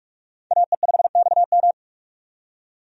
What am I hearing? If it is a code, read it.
Morse code, heard as a keyed tone, about 46 wpm, 700 Hz. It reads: AE5XM
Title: USA Call Signs - 5 Characters - 45wpm